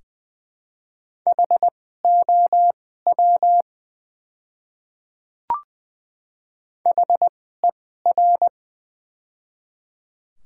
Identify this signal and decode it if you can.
Morse code, heard as a keyed tone, about 20 wpm, spaced as for 15 wpm, 700 Hz. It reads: HOW HER
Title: 2x Character Spacing - Top 100 Words - 20wpm